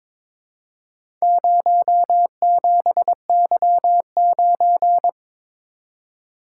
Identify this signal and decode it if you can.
Morse code, heard as a keyed tone, about 23 wpm, 700 Hz. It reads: 07Y9
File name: practice / 22wpm / Four Letter-Number Combinations - 22wpm